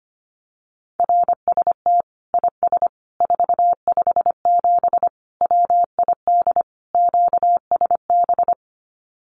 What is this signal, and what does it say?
Morse code, heard as a keyed tone, about 25 wpm, 700 Hz. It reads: RST IS 457 WID QSB
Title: Common QSO Phrases 25wpm